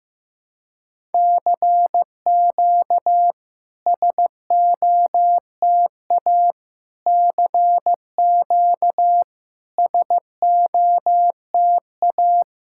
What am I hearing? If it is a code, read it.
Morse code, heard as a keyed tone, about 16 wpm, 700 Hz. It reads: CQ SOTA CQ SOTA
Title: SOTA QSO Practice 15wpm